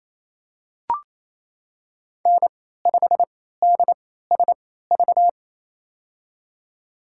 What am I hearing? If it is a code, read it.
Morse code, heard as a keyed tone, about 28 wpm, spaced as for 16 wpm, 700 Hz. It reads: N5DSV
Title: USA Call Signs - 3x Character Spacing - 28wpm